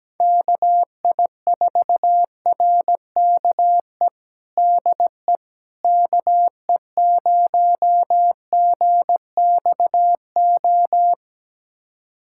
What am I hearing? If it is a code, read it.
Morse code, heard as a keyed tone, about 17 wpm, 700 Hz. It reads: KI4RKE DE KE0GXO
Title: USA Callsign DE Callsign - 6 Characters - 17wpm